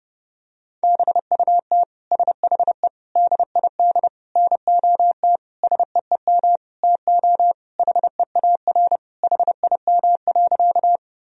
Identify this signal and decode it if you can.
Morse code, heard as a keyed tone, about 30 wpm, 700 Hz. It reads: BUT SHE DID NOT SEEM TO HEAR HIM.
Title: Sentences from Top 1000 Words - Review Part 1 - 30wpm